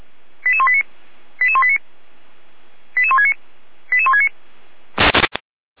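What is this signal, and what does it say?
Five-tone selective call (ZVEI), heard as quick runs of five tones, tones of about 70 ms. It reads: 88189 88189 88169 88169
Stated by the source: ZVEI five-tone selective call received on SDR.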